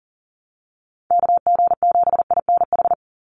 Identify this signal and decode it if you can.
Morse code, heard as a keyed tone, about 40 wpm, 700 Hz. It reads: KG7INH
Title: USA Call Signs 40wpm